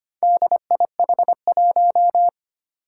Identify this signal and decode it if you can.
Morse code, heard as a keyed tone, about 25 wpm, 700 Hz. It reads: DIH1
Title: Four Challenging Letter-Number Combinations 25wpm